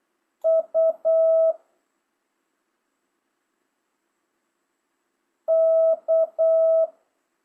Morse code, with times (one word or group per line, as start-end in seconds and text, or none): 0.5-1.5 U
5.5-6.9 K